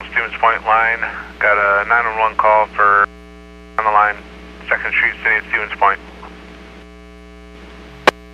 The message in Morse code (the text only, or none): none